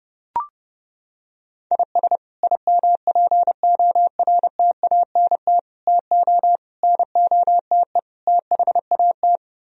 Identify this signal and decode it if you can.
Morse code, heard as a keyed tone, about 31 wpm, 700 Hz. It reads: IS IMPORTANT TO NOTE THAT